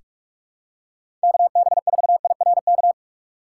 Morse code, heard as a keyed tone, about 43 wpm, 700 Hz. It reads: KB4IRK
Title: USA Call Signs - 6 Characters - 4x Character Spacing - 45wpm